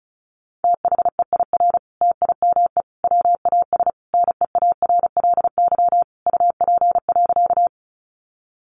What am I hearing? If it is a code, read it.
Morse code, heard as a keyed tone, about 34 wpm, 700 Hz. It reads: THEIR TIME WAS NEARLY UP.
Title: Sentences from Top 1300 Words 35wpm